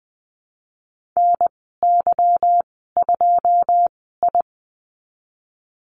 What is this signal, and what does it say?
Morse code, heard as a keyed tone, about 20 wpm, spaced as for 14 wpm, 700 Hz. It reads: NY2I
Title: 154 Lesson 25g - Letter p - Call Signs 20wpm